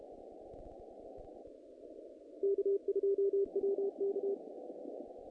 Morse code, heard as a keyed tone, about 30 wpm, 400 Hz. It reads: K2WK
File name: contests / naqp-cw-k5tr-op-k5pi- -2100